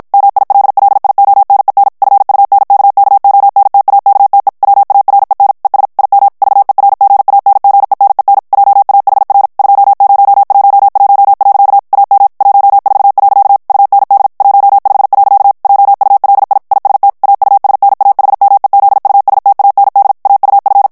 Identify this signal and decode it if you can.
Morse code, heard as a keyed tone, about 56 wpm, 800 Hz. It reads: MILLIONEN FUNKKONTAKTE WAREN ES IM VERGANGENEN JAHR 2011. AM 14. UND 15. JULI IST AUSNAHMEZUSTAND AUF
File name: morse mp3 / ALLEMAND texte clair 2 vitesse 50 mots minute